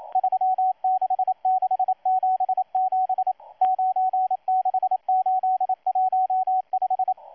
Morse code, heard as a keyed tone, about 27 wpm, 750 Hz, with radio noise. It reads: 36677 96815
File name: M12-2016-07-07-1310utc-13926khz-error-restart